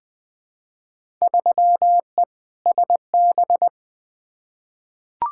Morse code, heard as a keyed tone, about 21 wpm, 700 Hz. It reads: 3E SB
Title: Field Day Exchanges Encore - 20wpm